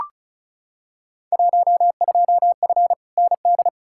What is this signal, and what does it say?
Morse code, heard as a keyed tone, about 36 wpm, 700 Hz. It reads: 12F ND
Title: Field Day Exchanges Encore - 35wpm